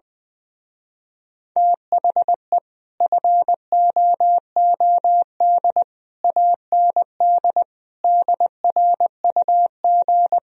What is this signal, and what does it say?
Morse code, heard as a keyed tone, about 20 wpm, 700 Hz. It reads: THE FOOD AND DRUG
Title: Speed-Racing - Sets of 5 Words 17wpm